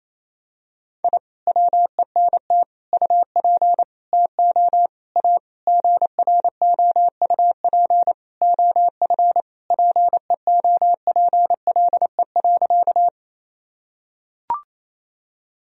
Morse code, heard as a keyed tone, about 27 wpm, 700 Hz. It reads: I WENT UP TO A GROUP OF PEOPLE.
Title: Sentences from Top 1000 Words - Review Part 1 - 28wpm